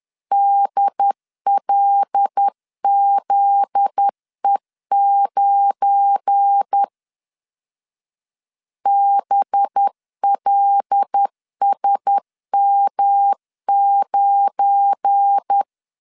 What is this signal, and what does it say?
Morse code, heard as a keyed tone, about 10 wpm, 800 Hz. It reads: DLZE9 BLSM9
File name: LEZIONI / 7 GF9Z / 7 GF9Z 10wpm 2611